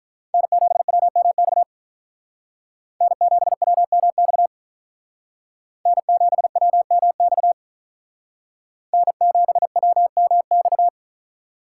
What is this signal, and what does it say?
Morse code, heard as a keyed tone, about 43 wpm, 700 Hz. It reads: N7WMX N7WMX N7WMX N7WMX
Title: Speed Racing - USA 1x3 Call Signs 35wpm